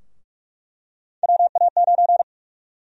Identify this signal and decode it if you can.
Morse code, heard as a keyed tone, about 43 wpm, 700 Hz. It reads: WA9
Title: USA Call Sign Prefixes - 45wpm